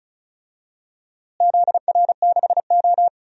Morse code, heard as a keyed tone, about 36 wpm, 700 Hz. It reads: ZR6O
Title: DXCC-100 Call Signs - 4 Characters - Repeated 2x - 35wpm